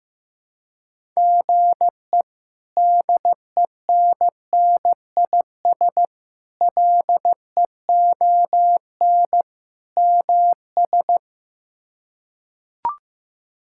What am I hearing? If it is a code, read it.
Morse code, heard as a keyed tone, about 15 wpm, 700 Hz. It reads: GE DENNIS LEON MS
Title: K1USN SST QSO Practice - 15wpm